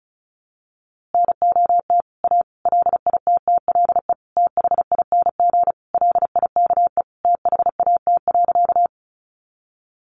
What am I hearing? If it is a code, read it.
Morse code, heard as a keyed tone, about 34 wpm, 700 Hz. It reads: NOT A LITTLE THING LIKE THAT.
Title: A Sound Of Thunder 35wpm